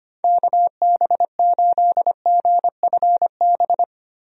Morse code, heard as a keyed tone, about 25 wpm, 700 Hz. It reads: KB8GFB
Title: USA Call Signs - 6 Characters - Repeated 2x - 25wpm